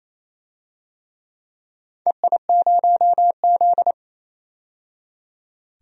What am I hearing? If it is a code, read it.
Morse code, heard as a keyed tone, about 28 wpm, 700 Hz. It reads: EI0Z
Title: DXCC-100 Call Signs - 4 Characters - 28wpm